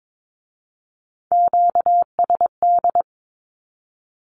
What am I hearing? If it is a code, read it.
Morse code, heard as a keyed tone, about 22 wpm, 700 Hz. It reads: QSD
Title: Q-Codes with Spoken Meaning - 22wpm